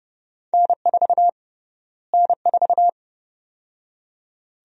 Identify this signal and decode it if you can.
Morse code, heard as a keyed tone, about 30 wpm, 700 Hz. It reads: N4 N4
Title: USA Call Sign Prefixes - Repeated 2x - 30wpm